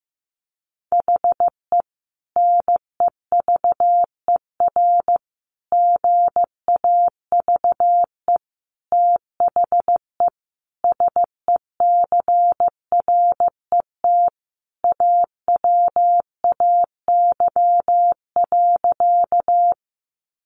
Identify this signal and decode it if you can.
Morse code, heard as a keyed tone, about 15 wpm, 700 Hz. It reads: HE NEVER GAVE THE SECRET AWAY.
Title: Sentences from Top 1200 Words 15wpm